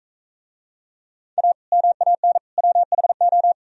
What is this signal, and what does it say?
Morse code, heard as a keyed tone, about 44 wpm, 700 Hz. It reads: A MAN WHO